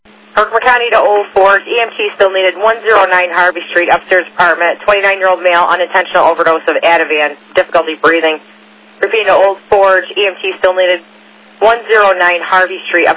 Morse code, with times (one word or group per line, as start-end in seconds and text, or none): none